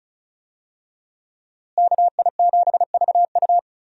Morse code, heard as a keyed tone, about 36 wpm, 700 Hz. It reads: KI7VU
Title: USA Call Signs - 5 Characters - Repeated 2x - 35wpm